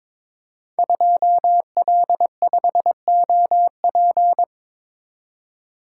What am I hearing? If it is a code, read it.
Morse code, heard as a keyed tone, about 22 wpm, 700 Hz. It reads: 2L5OP
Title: Code Groups - 5 Letter:Numbers - 22wpm